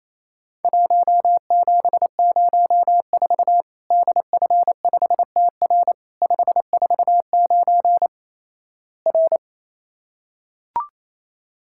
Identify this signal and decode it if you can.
Morse code, heard as a keyed tone, about 28 wpm, 700 Hz. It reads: 1704 DF5TR 549 R
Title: Worked All Europe DX Contest - QTC Exchange - 28wpm